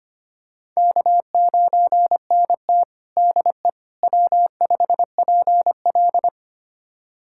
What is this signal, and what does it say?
Morse code, heard as a keyed tone, about 25 wpm, 700 Hz. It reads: K9NT DE W5PL
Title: USA Callsign DE Callsign - 4 Characters - 25wpm